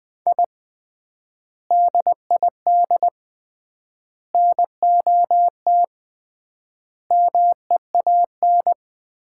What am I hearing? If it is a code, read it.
Morse code, heard as a keyed tone, about 21 wpm, 700 Hz. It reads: I DID NOT MEAN
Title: 3x Word-Spacing - Sets of 5 Words - 20wpm